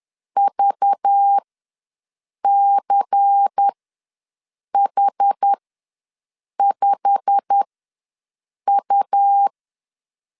Morse code, heard as a keyed tone, about 10 wpm, spaced as for 6 wpm, 800 Hz. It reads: VCH5U
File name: LEZIONI / 10 SIGNS / 10D COMA ws 10wpm 4314